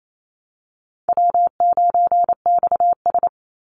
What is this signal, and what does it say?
Morse code, heard as a keyed tone, about 28 wpm, 700 Hz. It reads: W9XS
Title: Call Signs - 28wpm